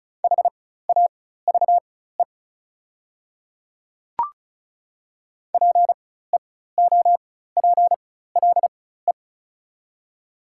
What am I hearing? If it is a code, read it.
Morse code, heard as a keyed tone, about 36 wpm, spaced as for 16 wpm, 690 Hz. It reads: HAVE PEOPLE
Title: 4x Character Spacing - Top 100 Words - 35wpm